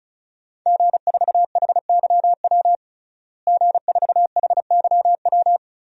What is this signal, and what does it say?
Morse code, heard as a keyed tone, about 36 wpm, 700 Hz. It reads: G4HYW G4HYW
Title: DXCC-100 Call Signs - 5 Characters - Repeated 3x - 35wpm